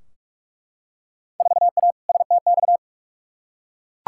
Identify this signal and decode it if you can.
Morse code, heard as a keyed tone, about 42 wpm, 700 Hz. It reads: 4A STX
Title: Field Day Exchanges - 45wpm